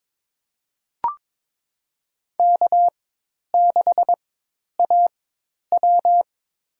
Morse code, heard as a keyed tone, about 22 wpm, spaced as for 10 wpm, 700 Hz. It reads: K6AW
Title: USA Call Signs - 4 Characters - 4x Character Spacing - 22wpm